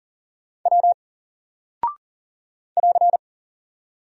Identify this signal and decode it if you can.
Morse code, heard as a keyed tone, about 40 wpm, 700 Hz. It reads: W <AR>
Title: Single Character (Letter:Number:Punctuation:Prosign) Alt 40wpm